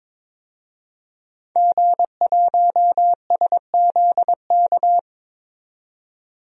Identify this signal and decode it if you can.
Morse code, heard as a keyed tone, about 22 wpm, 700 Hz. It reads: G1SZK
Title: DXCC-100 Call Signs - 5 Characters - 22wpm